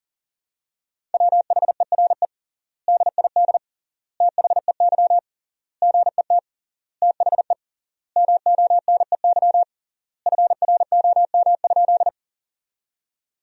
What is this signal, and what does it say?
Morse code, heard as a keyed tone, about 40 wpm, 700 Hz. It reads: WHERE DID THEY GET THE MONEY FROM?
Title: 3x Word-Spacing - Sentences from Top 400 Words - 40wpm